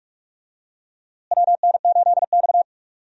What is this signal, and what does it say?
Morse code, heard as a keyed tone, about 46 wpm, 700 Hz. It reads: WN8X
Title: USA Call Signs - 4 Characters - Repeated 3x - 45wpm